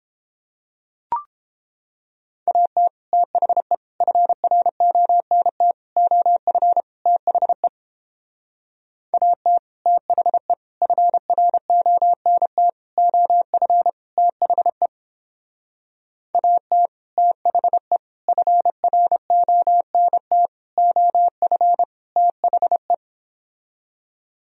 Morse code, heard as a keyed tone, about 31 wpm, 700 Hz. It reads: AT THE FRONT OF THE AT THE FRONT OF THE AT THE FRONT OF THE